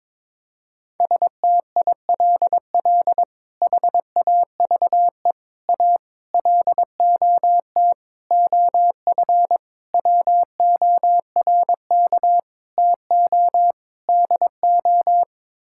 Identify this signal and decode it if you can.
Morse code, heard as a keyed tone, about 22 wpm, 700 Hz. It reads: STILL HAVE A LOT OF WORK TO DO
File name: practice / 22wpm / Sets of 8 Words - 22wpm